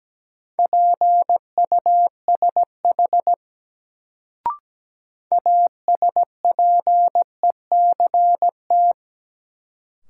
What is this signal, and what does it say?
Morse code, heard as a keyed tone, about 17 wpm, 700 Hz. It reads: PUSH ASPECT
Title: Sentences from Top 1300 Words - New Word Drill - 17wpm